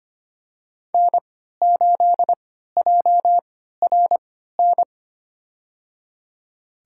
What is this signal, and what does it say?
Morse code, heard as a keyed tone, about 25 wpm, spaced as for 14 wpm, 700 Hz. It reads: N8JRN
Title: USA Call Signs - 5 Characters - 3x Character Spacing - 25wpm